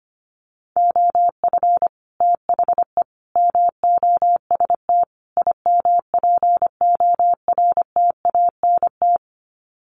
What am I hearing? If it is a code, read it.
Morse code, heard as a keyed tone, about 25 wpm, 700 Hz. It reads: OF THE MOST IMPORTANT
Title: Sets of 4 Words 25wpm